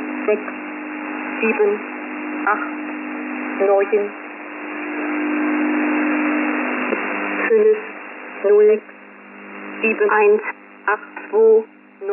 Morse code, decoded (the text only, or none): none